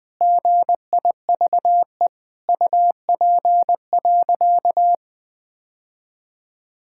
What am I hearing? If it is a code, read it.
Morse code, heard as a keyed tone, about 20 wpm, 700 Hz. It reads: GIVE UP.
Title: Sentences with 3-Word Phrases - Spoken Beginning - 20wpm